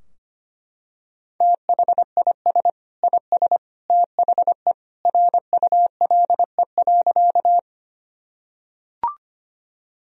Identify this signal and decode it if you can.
Morse code, heard as a keyed tone, about 25 wpm, 700 Hz. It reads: THIS IS THE RULE.
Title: Sentences from Top 1000 Words - Review Part 2 - 25wpm